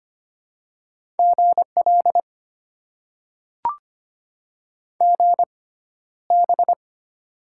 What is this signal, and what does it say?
Morse code, heard as a keyed tone, about 25 wpm, spaced as for 13 wpm, 700 Hz. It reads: GL GB
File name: Top 90 QSO Elements - 6x Character Spacing - 25wpm